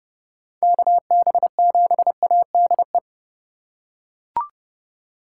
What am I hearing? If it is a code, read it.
Morse code, heard as a keyed tone, about 30 wpm, 700 Hz. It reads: KB7ADE
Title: USA Call Signs - 4x Character Spacing - 30wpm